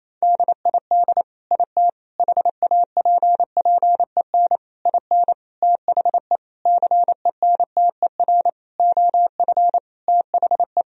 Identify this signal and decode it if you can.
Morse code, heard as a keyed tone, about 28 wpm, 700 Hz. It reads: DID IT HAPPEN IN THE CENTER OF THE
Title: Sentences from Top 1000 Words - Review Part 2 - 28wpm